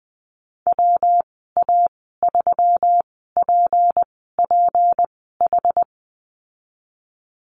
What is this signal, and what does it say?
Morse code, heard as a keyed tone, about 20 wpm, spaced as for 15 wpm, 700 Hz. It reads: WA3PPH
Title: 213 Lesson 34g - Number 8 - Call Signs 20wpm